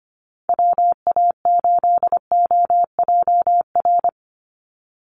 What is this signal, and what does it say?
Morse code, heard as a keyed tone, about 25 wpm, 700 Hz. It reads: WA8OJR
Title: Call Signs Repeated 2x - 25wpm